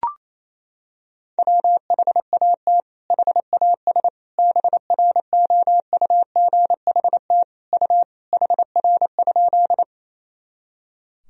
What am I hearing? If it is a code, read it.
Morse code, heard as a keyed tone, about 28 wpm, 700 Hz. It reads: WHAT HAS BROUGHT U HR?
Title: Sentences from Top 500 Words and Abbreviations 28wpm